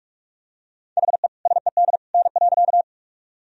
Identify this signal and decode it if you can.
Morse code, heard as a keyed tone, about 41 wpm, 700 Hz. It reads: HE SED N.
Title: Sentences from Top 100 Words and Abbreviations 45wpm